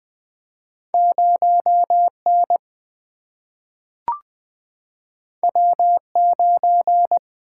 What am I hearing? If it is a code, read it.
Morse code, heard as a keyed tone, about 20 wpm, 700 Hz. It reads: 0 N W 9
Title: Single-Letter Single-Number 20wpm